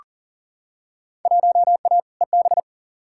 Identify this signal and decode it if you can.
Morse code, heard as a keyed tone, about 40 wpm, 700 Hz. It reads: 1A EB
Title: Field Day Exchanges - 40wpm